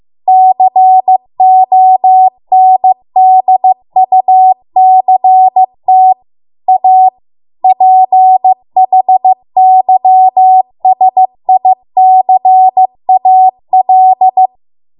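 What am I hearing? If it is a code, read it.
Morse code, heard as a keyed tone, about 15 wpm, 750 Hz. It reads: CONDUCT A PHYSICAL
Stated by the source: Morse code practice at 15 wpm from the national amateur radio association's code archive.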